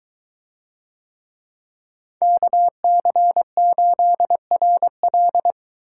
Morse code, heard as a keyed tone, about 22 wpm, 700 Hz. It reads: KC8RL